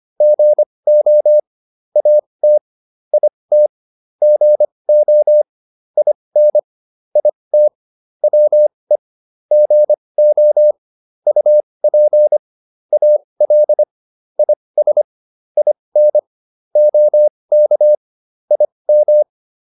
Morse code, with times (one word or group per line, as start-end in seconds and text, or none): 0.2-1.4 GO
2.0-2.6 AT
3.1-3.7 IT
4.2-5.4 GO
6.0-6.6 IN
7.2-7.7 IT
8.2-9.0 WE
9.5-10.7 GO
11.3-12.4 UP
12.9-13.8 AL
14.4-15.0 IS
15.6-16.2 IN
16.8-17.9 OK
18.5-19.2 IM